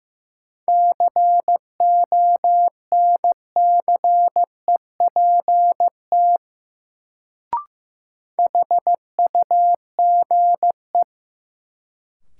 Morse code, heard as a keyed tone, about 15 wpm, 700 Hz. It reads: CONCEPT HUGE
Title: Sentences from Top 1100 Words - New Word Drill Alt - 15wpm